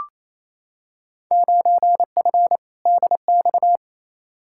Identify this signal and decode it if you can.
Morse code, heard as a keyed tone, about 28 wpm, 700 Hz. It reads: 9F DX
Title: Field Day Exchanges Encore - 28wpm